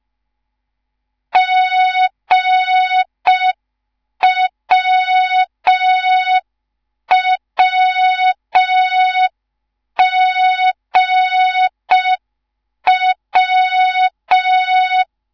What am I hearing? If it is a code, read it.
Morse code, heard as a keyed tone, about 5 wpm, 3700 Hz. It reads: GWWGW
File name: Morze-HA2MN / hang / 10GWZ HA2MN